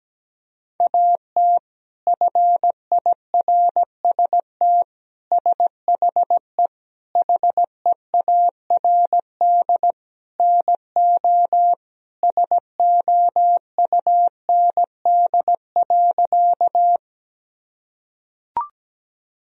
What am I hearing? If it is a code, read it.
Morse code, heard as a keyed tone, about 17 wpm, 700 Hz. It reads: AT FIRST SHE HEARD NO SOUND.
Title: Sentences from Top 1000 Words - Review Part 1 - 17wpm